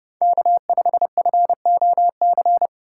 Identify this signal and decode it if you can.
Morse code, heard as a keyed tone, about 32 wpm, 700 Hz. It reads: K5FOC
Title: USA Call Signs - 5 Characters - 4x Character Spacing - 30wpm